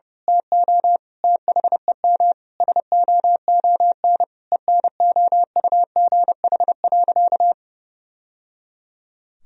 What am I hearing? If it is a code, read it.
Morse code, heard as a keyed tone, about 32 wpm, 700 Hz. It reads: TO THEM SOON ENOUGH.
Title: Sentences from Top 1000 Words - Review Part 3 - 30wpm